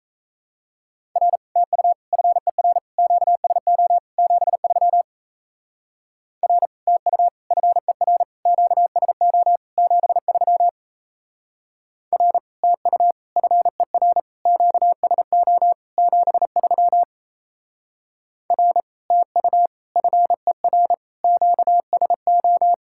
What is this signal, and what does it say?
Morse code, heard as a keyed tone, about 35 wpm, 700 Hz. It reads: R TU FER QSO 73 R TU FER QSO 73 R TU FER QSO 73 R TU FER QSO
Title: Speed-Racing Common QSO Phrases - 28wpm